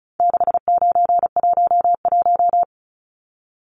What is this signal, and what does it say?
Morse code, heard as a keyed tone, about 34 wpm, 700 Hz. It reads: 6911
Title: Numbers - 4-Digit 35wpm